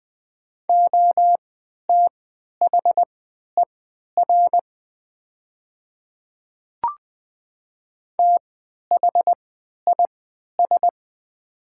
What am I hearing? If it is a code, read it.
Morse code, heard as a keyed tone, about 20 wpm, spaced as for 11 wpm, 700 Hz. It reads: OTHER THIS
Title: 3x Character Spacing - Top 100 Words - 20wpm